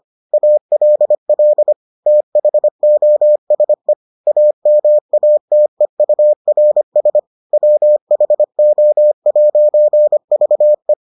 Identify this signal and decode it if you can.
Morse code, heard as a keyed tone, about 25 wpm, 600 Hz. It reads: ALL THOSE AMATEURS WHO'VE